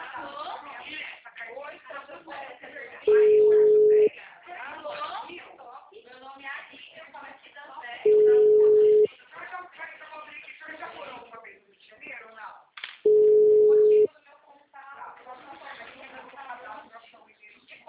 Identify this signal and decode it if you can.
Ringback tone (Europe, 425 Hz), with speech.
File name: out- -2025- - - .51734